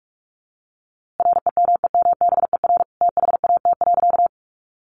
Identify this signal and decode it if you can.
Morse code, heard as a keyed tone, about 44 wpm, 700 Hz. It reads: REMEMBER THAT.
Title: A Sound Of Thunder 45wpm